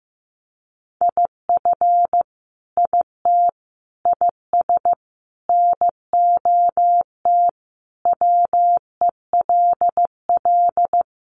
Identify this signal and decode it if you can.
Morse code, heard as a keyed tone, about 15 wpm, 700 Hz. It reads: IF IT IS NOT WELL
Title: Sentences from Top 1200 Words 15wpm